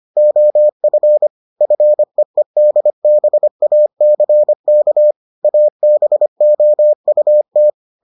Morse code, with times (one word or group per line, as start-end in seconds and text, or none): 0.2-1.3 OF
1.6-5.1 FEEDBACK
5.4-7.7 ABOUT